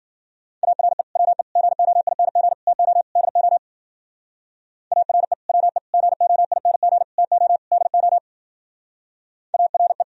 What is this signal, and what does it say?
Morse code, heard as a keyed tone, about 50 wpm, 700 Hz. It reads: ARE WE GOING TO DO ARE WE GOING TO DO ARE
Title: Speed-Racing - Sets of 5 Words 40wpm